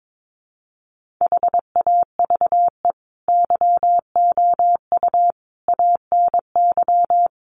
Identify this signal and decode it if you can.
Morse code, heard as a keyed tone, about 22 wpm, 700 Hz. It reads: HAVE YOU ANY
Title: Sentences from Top 1400 Words 22wpm